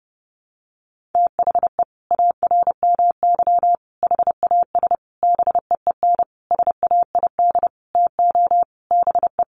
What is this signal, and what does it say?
Morse code, heard as a keyed tone, about 30 wpm, 700 Hz. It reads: THE ARMY HAS BEEN SAID TO BE
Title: Sentences from Top 1400 Words 30wpm